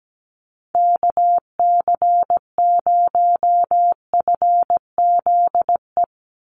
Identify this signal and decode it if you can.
Morse code, heard as a keyed tone, about 17 wpm, 700 Hz. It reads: KC0FZE
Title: USA Call Signs 17wpm